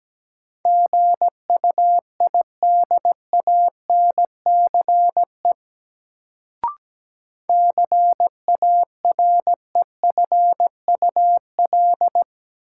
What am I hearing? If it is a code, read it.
Morse code, heard as a keyed tone, about 17 wpm, 700 Hz. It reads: GUIDANCE CAREFUL
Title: Sentences from Top 1600 Words - New Word Drill - 17wpm